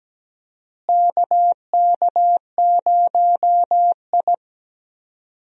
Morse code, heard as a keyed tone, about 17 wpm, 700 Hz. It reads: KK0I
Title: USA Call Signs - 4 Characters - Repeated 3x - 17wpm